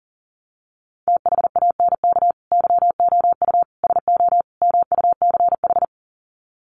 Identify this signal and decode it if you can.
Morse code, heard as a keyed tone, about 40 wpm, 700 Hz. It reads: THANK YOU SO MUCH
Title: Sets of 4 Words 40wpm